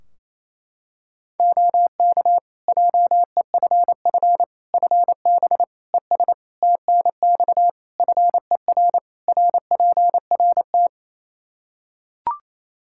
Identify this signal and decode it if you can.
Morse code, heard as a keyed tone, about 27 wpm, 700 Hz. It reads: OK JEFF FB ES TNX FER RPRT
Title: K1USN SST QSO Practice 28wpm